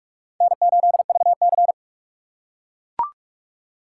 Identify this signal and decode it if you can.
Morse code, heard as a keyed tone, about 42 wpm, 700 Hz. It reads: N8VC